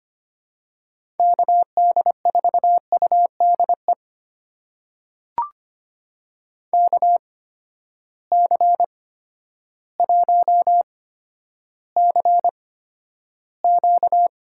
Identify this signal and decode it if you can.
Morse code, heard as a keyed tone, about 25 wpm, spaced as for 11 wpm, 700 Hz. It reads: KD4UDE KC1CQ
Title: USA Call Signs - 6 Characters - 8x Character Spacing - 25wpm